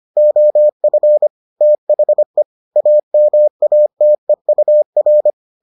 Morse code, heard as a keyed tone, about 25 wpm, 600 Hz. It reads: OF THE AMATEUR